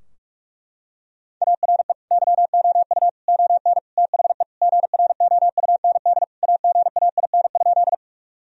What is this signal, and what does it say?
Morse code, heard as a keyed tone, about 41 wpm, 700 Hz. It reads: ARE YOU ON THE GROUND AGAIN?
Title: Sentences from Top 1000 Words - Review Part 2 - 45wpm